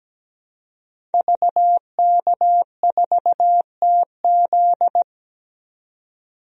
Morse code, heard as a keyed tone, about 17 wpm, 700 Hz. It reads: VK4TZ
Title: DXCC-100 Call Signs - 5 Characters - 17wpm